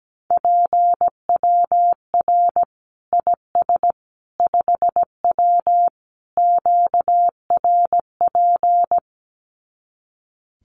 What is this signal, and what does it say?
Morse code, heard as a keyed tone, about 17 wpm, 700 Hz. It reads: PWR IS 5W QRP
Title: Common QSO Phrases 17wpm